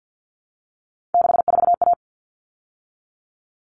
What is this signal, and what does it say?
Morse code, heard as a keyed tone, about 50 wpm, 700 Hz. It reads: 64A
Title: All Cut Numbers - 3-Digit 50wpm